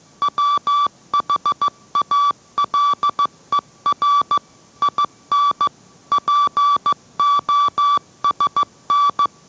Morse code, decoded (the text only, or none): WHALER IN POSN